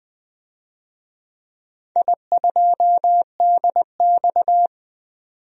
Morse code, heard as a keyed tone, about 20 wpm, 700 Hz. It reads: I2DX
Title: DXCC-100 Call Signs - 4 Characters - 20wpm